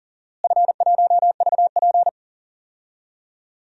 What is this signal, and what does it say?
Morse code, heard as a keyed tone, about 40 wpm, 700 Hz. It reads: F1VP
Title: DXCC-100 Call Signs - 4 Characters - Repeated 2x - 40wpm